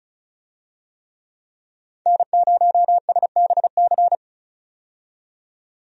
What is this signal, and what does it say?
Morse code, heard as a keyed tone, about 34 wpm, 700 Hz. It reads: N0SBC
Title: USA Call Signs - 5 Characters - 35wpm